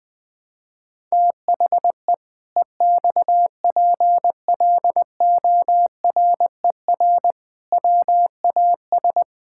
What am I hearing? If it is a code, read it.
Morse code, heard as a keyed tone, about 20 wpm, 700 Hz. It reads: THE EXPLORER WAS
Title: Pangram Challenge v2 - 20wpm